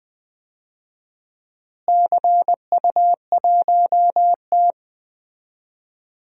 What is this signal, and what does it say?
Morse code, heard as a keyed tone, about 20 wpm, 700 Hz. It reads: CU1T
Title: DXCC-100 Call Signs - 4 Characters - Repeated 3x - 20wpm